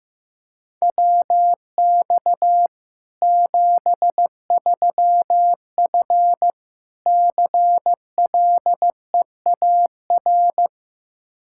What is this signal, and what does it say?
Morse code, heard as a keyed tone, about 15 wpm, 700 Hz. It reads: WX 73F CLEAR
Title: Practice QSO - Exchange 2 - Info - 15wpm